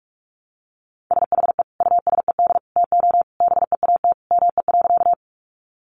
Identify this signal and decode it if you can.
Morse code, heard as a keyed tone, about 44 wpm, 700 Hz. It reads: SHE USED TO BEAT ME.
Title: Sentences from Top 1300 Words 45wpm